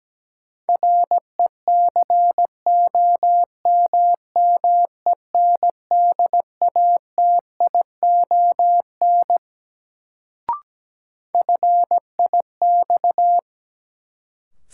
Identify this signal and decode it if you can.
Morse code, heard as a keyed tone, about 17 wpm, 700 Hz. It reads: RECOMMENDATION FIX
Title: Sentences from Top 1400 Words - New Word Drill Alt - 17wpm